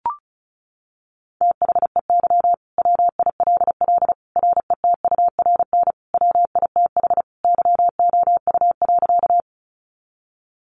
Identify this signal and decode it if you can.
Morse code, heard as a keyed tone, about 34 wpm, 700 Hz. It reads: THEY WILL RETURN WITH YOU.
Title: Sentences from Top 800 Words 35wpm